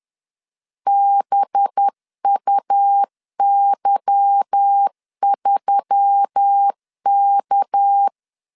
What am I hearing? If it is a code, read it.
Morse code, heard as a keyed tone, about 10 wpm, 800 Hz. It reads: BUY3K
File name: LEZIONI / 8 YJ45 / 8 Y 10wpm 2615